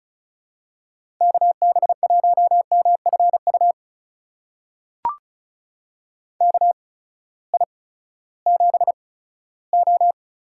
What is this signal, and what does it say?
Morse code, heard as a keyed tone, about 36 wpm, spaced as for 18 wpm, 700 Hz. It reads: KB1MFU KI7O
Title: USA Call Signs - 6 Characters - 8x Character Spacing - 35wpm